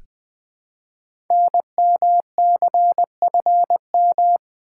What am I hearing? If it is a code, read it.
Morse code, heard as a keyed tone, about 20 wpm, 700 Hz. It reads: NMCFM
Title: Code Groups - 5 Letters - 20wpm